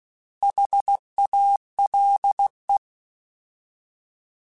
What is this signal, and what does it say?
Morse code, heard as a keyed tone, about 16 wpm, 800 Hz. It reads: HALE
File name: Alphabetical / Alphabetical mp3s / Hs @15